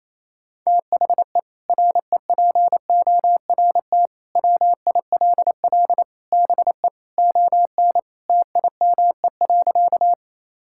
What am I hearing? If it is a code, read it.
Morse code, heard as a keyed tone, about 28 wpm, 700 Hz. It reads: THE REPORT WILL BE ON TIME.
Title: Sentences from Top 1000 Words - Review Part 6 - 28wpm